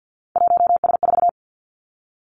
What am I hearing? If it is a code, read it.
Morse code, heard as a keyed tone, about 50 wpm, 700 Hz. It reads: 1S4
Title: Three Challenging Letter-Number Combinations 50wpm